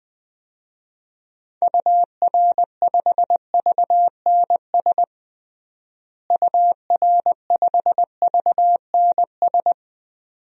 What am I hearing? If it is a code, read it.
Morse code, heard as a keyed tone, about 21 wpm, 700 Hz. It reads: UR5VNS UR5VNS
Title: DXCC-100 Call Signs - 6 Characters - Repeated 2x - 20wpm